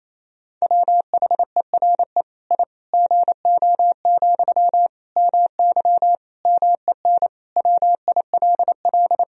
Morse code, heard as a keyed tone, about 28 wpm, 700 Hz. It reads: WHERE I GO, MY MEN WILL
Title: Sentences from Top 1000 Words - Review Part 1 - 28wpm